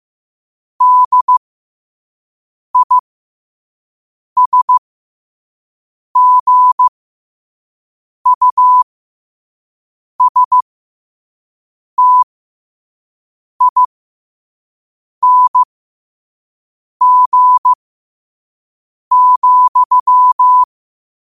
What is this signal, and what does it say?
Morse code, heard as a keyed tone, about 15 wpm, spaced as for 5 wpm, 1000 Hz. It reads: DISGUSTING,
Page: New quotes every day in morse code at 5 Words per minute.